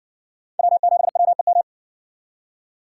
Morse code, heard as a keyed tone, about 54 wpm, 700 Hz. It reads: W7WEM